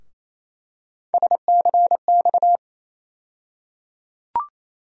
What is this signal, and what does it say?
Morse code, heard as a keyed tone, about 28 wpm, 700 Hz. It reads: SCX
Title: USA Call Sign Postfixes - 3 Letters 28wpm